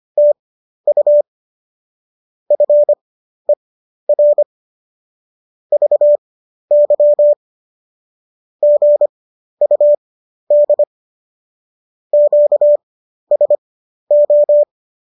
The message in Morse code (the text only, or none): TU FER VY GUD QSO